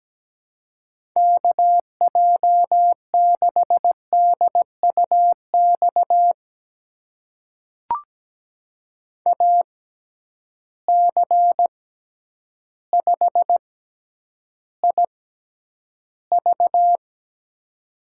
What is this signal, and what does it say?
Morse code, heard as a keyed tone, about 17 wpm, spaced as for 9 wpm, 700 Hz. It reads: KJ6DUX AC5IV
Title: USA Call Signs - 6x Character Spacing - 17wpm